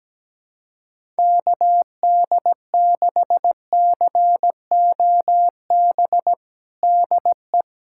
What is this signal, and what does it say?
Morse code, heard as a keyed tone, about 17 wpm, 700 Hz. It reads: KD6COB DE
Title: USA Callsign DE Callsign - 6 Characters - 17wpm